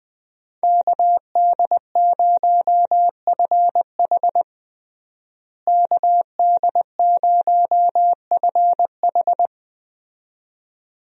Morse code, heard as a keyed tone, about 20 wpm, 700 Hz. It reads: KD0FH KD0FH
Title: USA Call Signs - 5 Characters - Repeated 2x - 20wpm